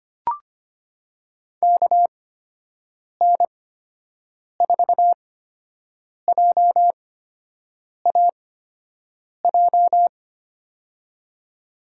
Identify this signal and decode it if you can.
Morse code, heard as a keyed tone, about 25 wpm, spaced as for 7 wpm, 700 Hz. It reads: KN4JAJ
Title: USA Call Signs - 6 Characters - 8x Character Spacing - 25wpm